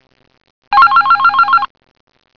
There is no code.